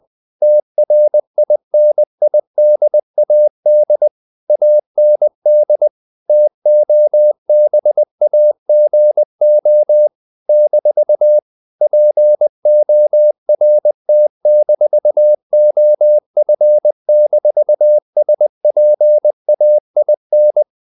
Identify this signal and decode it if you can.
Morse code, heard as a keyed tone, about 20 wpm, 600 Hz. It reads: TRINIDAD AND TOBAGO - PORT-OF-SPAIN
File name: CW / Practice / NationalCapitals20